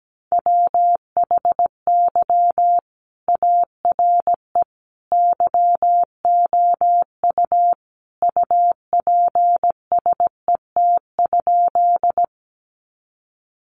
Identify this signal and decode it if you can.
Morse code, heard as a keyed tone, about 17 wpm, 700 Hz. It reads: WHY ARE YOU UPSET?
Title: Sentences from Top 800 Words 17wpm